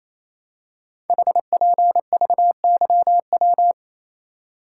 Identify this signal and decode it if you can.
Morse code, heard as a keyed tone, about 27 wpm, 700 Hz. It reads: HPVYW
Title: Code Groups - 5 Letters - 28wpm